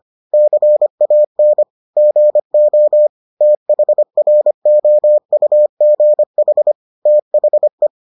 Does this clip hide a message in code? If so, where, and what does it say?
Yes: Morse code from 0.3 to 7.9 s, heard as a keyed tone, about 25 wpm, 600 Hz. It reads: CAN GO THROUGH THE